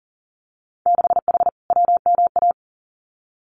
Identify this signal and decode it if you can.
Morse code, heard as a keyed tone, about 42 wpm, 700 Hz. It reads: 6H WMA
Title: Winter Field Day Exchanges Encore 40wpm